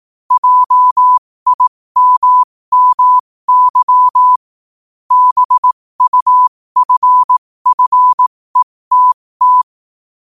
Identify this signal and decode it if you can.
Morse code, heard as a keyed tone, about 18 wpm, spaced as for 15 wpm, 1000 Hz. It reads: JIMMY BUFFETT
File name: qotdfiles / 15wpm